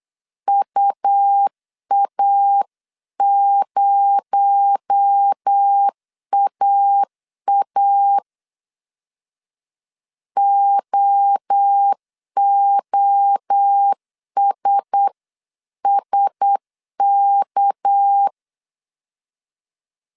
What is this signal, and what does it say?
Morse code, heard as a keyed tone, about 8 wpm, 800 Hz. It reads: UA0AA OOSSK
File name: CW / DET / 1 9 1500C S C 8wpm 5653